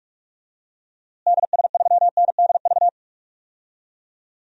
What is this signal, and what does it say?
Morse code, heard as a keyed tone, about 45 wpm, 700 Hz. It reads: DS3NDV
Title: DXCC-100 Call Signs - 6 Characters - Repeated 2x - 45wpm